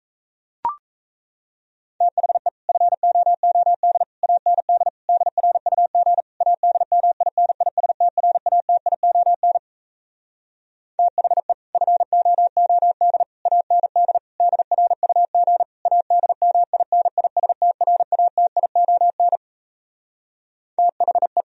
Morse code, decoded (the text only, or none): THE FOOD AND DRUG ADMINISTRATION THE FOOD AND DRUG ADMINISTRATION THE